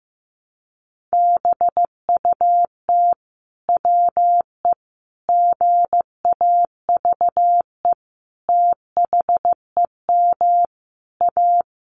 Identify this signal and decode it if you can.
Morse code, heard as a keyed tone, about 15 wpm, 700 Hz. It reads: BUT WE GAVE THEM A
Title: Sentences from Top 1200 Words 15wpm